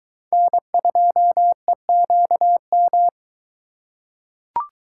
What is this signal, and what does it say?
Morse code, heard as a keyed tone, about 21 wpm, 700 Hz. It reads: N2EQM